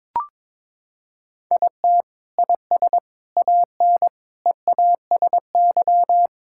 Morse code, heard as a keyed tone, about 22 wpm, 700 Hz. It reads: IT IS AN EASY
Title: Sentences from Top 1000 Words - Review Part 6 - 22wpm